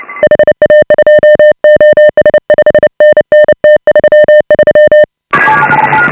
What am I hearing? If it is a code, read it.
Morse code, heard as a keyed tone, about 31 wpm, 600 Hz. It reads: HA2OS 5NN T33
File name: AUDIO2018 / HA5BA CW